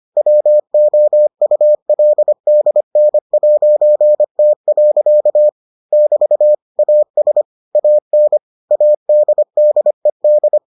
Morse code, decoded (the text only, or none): WOULDN'T. <BT> AS AN ADDED